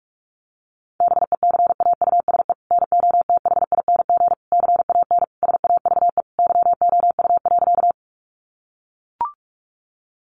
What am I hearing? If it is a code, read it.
Morse code, heard as a keyed tone, about 44 wpm, 700 Hz. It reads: BECAUSE NOTHING CAN SAVE YOU.
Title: Sentences from Top 1100 Words 45wpm